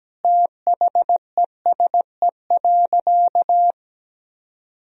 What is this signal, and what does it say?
Morse code, heard as a keyed tone, about 17 wpm, 700 Hz. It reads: THESE.
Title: Sentences from Top 1000 Words - Review Part 4 - 17wpm